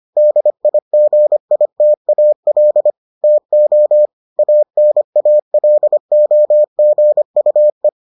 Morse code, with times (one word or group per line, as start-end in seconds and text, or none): 0.2-2.9 DIGITAL
3.2-4.1 TO
4.4-7.9 ANALOGUE